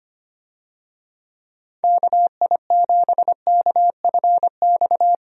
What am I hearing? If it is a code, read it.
Morse code, heard as a keyed tone, about 25 wpm, 700 Hz. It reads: KI7KFX
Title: USA Call Signs - 6 Characters - Repeated 3x - 25wpm